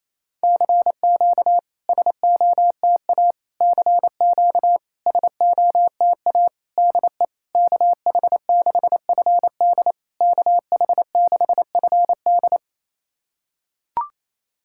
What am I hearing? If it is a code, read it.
Morse code, heard as a keyed tone, about 28 wpm, 660 Hz. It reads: CQ SOTA CQ SOTA DE KH6FD KH6FD
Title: SOTA QSO Practice 28wpm